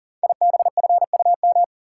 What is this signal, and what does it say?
Morse code, heard as a keyed tone, about 38 wpm, 700 Hz. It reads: IBFUM
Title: Code Groups - 5 Letter:Numbers - 40wpm